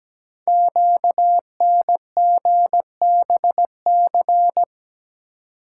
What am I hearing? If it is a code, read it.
Morse code, heard as a keyed tone, about 17 wpm, 700 Hz. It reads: QNGBC
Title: Code Groups - 5 Letters - 17wpm